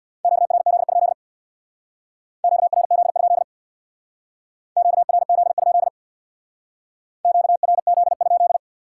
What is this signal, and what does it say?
Morse code, heard as a keyed tone, about 65 wpm, 700 Hz. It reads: QRZ? QRZ? QRZ? QRZ?